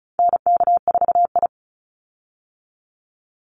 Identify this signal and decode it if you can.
Morse code, heard as a keyed tone, about 34 wpm, 700 Hz. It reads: NK4I
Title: Call Signs Repeated 3x - 35wpm